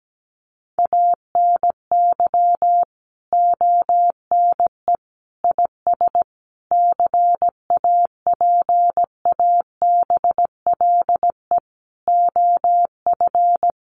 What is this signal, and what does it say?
Morse code, heard as a keyed tone, about 17 wpm, 700 Hz. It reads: ANY ONE IS CAPABLE OF
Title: Sentences from Top 1200 Words 17wpm